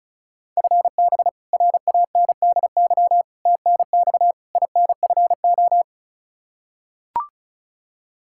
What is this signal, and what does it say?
Morse code, heard as a keyed tone, about 36 wpm, 700 Hz. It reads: FB RANDY TNX INFO
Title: Practice QSO - Exchange 3 - More Info - 35wpm